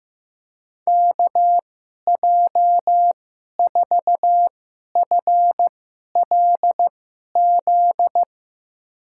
Morse code, heard as a keyed tone, about 15 wpm, spaced as for 11 wpm, 700 Hz. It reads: KJ4FLZ
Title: USA Call Signs - 6 Characters - 2x Character Spacing - 15wpm